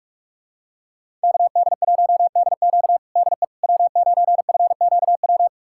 Morse code, heard as a keyed tone, about 42 wpm, 700 Hz. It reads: KD1DQ DE W9FQW
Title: USA Callsign DE Callsign - 5 Characters - 45wpm